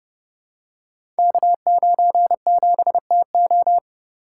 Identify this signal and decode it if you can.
Morse code, heard as a keyed tone, about 30 wpm, 700 Hz. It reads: K97TO
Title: Code Groups - 5 Letter:Numbers - 30wpm